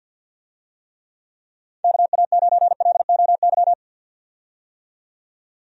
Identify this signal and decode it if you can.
Morse code, heard as a keyed tone, about 50 wpm, 700 Hz. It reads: KA9LOY
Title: USA Call Signs - 6 Characters - 50wpm